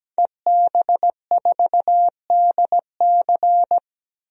Morse code, heard as a keyed tone, about 17 wpm, 700 Hz. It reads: EB4DC
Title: DXCC-100 Call Signs - 5 Characters - 17wpm